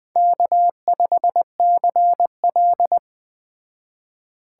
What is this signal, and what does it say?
Morse code, heard as a keyed tone, about 20 wpm, 700 Hz. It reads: K5CL